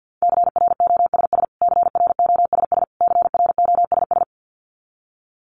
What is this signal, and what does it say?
Morse code, heard as a keyed tone, about 50 wpm, 700 Hz. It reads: CROSS CROSS CROSS
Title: Top 500 Words Repeated 50wpm